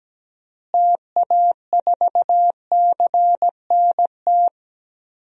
Morse code, heard as a keyed tone, about 17 wpm, 700 Hz. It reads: TA4CNT
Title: DXCC-100 Call Signs - 6 Characters - Repeated 2x - 17wpm